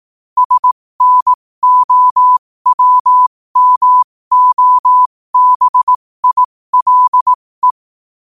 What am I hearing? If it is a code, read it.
Morse code, heard as a keyed tone, about 18 wpm, spaced as for 15 wpm, 1000 Hz. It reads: SNOWMOBILE
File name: qotdfiles / 15wpm